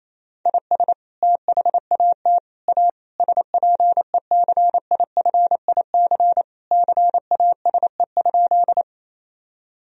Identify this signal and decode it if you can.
Morse code, heard as a keyed tone, about 28 wpm, 700 Hz. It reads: IS THAT A SPECIFIC CASE?
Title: Sentences from Top 1000 Words - Review Part 3 - 28wpm